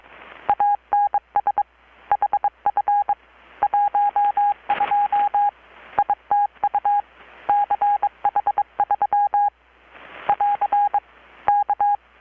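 Morse code, heard as a keyed tone, about 22 wpm, 820 Hz. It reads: ANS HF 12 ITU CH3 <AR> K
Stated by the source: Morse from KPH's Night of Nights XX maritime radiotelegraphy event, received in Chiapas, Mexico about 3,800 km away on a Kenwood R-600 with a grounded 10 m long-wire antenna.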